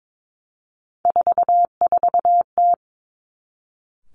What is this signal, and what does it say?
Morse code, heard as a keyed tone, about 22 wpm, 700 Hz. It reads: 44T
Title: Three Challenging Letter-Number Combinations 22wpm